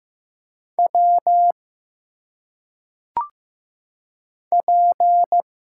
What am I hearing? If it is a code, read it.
Morse code, heard as a keyed tone, about 15 wpm, 700 Hz. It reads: W P
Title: Single Letters - Phonetic Pronunciation 15wpm